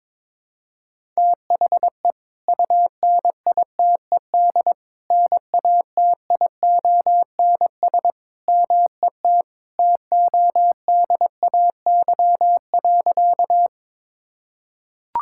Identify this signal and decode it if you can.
Morse code, heard as a keyed tone, about 22 wpm, 700 Hz. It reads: THE UNITED NATIONS MET TODAY.
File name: practice / 22wpm / Sentences with 3-Word Phrases - 22wpm